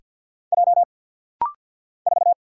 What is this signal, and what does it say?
Morse code, heard as a keyed tone, about 50 wpm, 690 Hz. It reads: J 4
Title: Single Letter-Number Alt - Rapid-Fire 50wpm